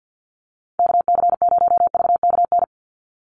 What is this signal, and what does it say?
Morse code, heard as a keyed tone, about 50 wpm, 700 Hz. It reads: KC0VKN
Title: Call Signs - 50wpm